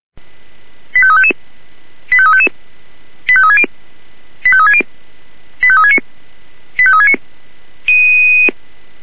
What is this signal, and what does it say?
Five-tone selective call (ZVEI), heard as quick runs of five tones, tones of about 70 ms. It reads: 85355 85355 85360 85360 85369 85369